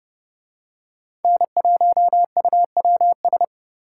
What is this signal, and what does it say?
Morse code, heard as a keyed tone, about 29 wpm, 700 Hz. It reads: N1UWS